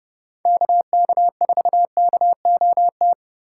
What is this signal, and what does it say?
Morse code, heard as a keyed tone, about 30 wpm, 700 Hz. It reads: KK4KOT
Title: USA Call Signs - 6 Characters - 3x Character Spacing - 30wpm